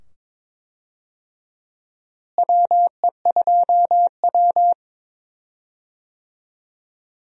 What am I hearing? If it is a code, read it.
Morse code, heard as a keyed tone, about 21 wpm, 700 Hz. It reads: WE2W
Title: USA Call Signs - 4 Characters - Repeated 2x - 22wpm